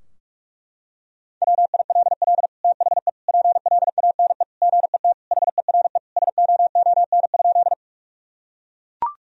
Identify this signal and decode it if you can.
Morse code, heard as a keyed tone, about 42 wpm, 700 Hz. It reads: WILL THE PLANE GET HERE SOON?
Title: Sentences from Top 1000 Words - Review Part 3 - 45wpm